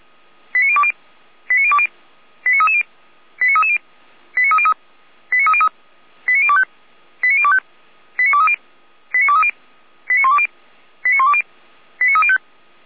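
Five-tone selective call (ZVEI), heard as quick runs of five tones, tones of about 70 ms. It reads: 89020 89020 89330 89330 89303 89303 89025 89025 89230 89230 89120 89120 89305
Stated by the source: ZVEI five-tone selective call 89315 received by SDR, from the audio server of the Feuerwehr Orlamünde fire brigade.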